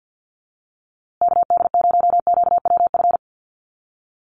Morse code, heard as a keyed tone, about 50 wpm, 700 Hz. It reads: KD0QWF
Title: USA Call Signs Repeated 3x 50wpm